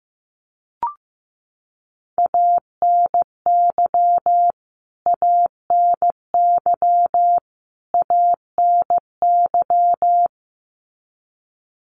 Morse code, heard as a keyed tone, about 15 wpm, 700 Hz. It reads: ANY ANY ANY
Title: Top 200 Words Repeated 15wpm